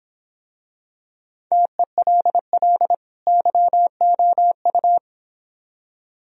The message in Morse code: TELL YOU

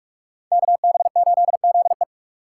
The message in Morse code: KB8ZE

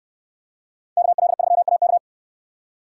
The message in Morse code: KL3AK

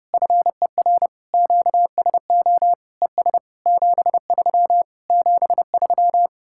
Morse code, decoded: FER QSO ES 73 73

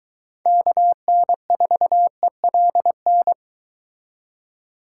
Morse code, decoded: KN4ELN

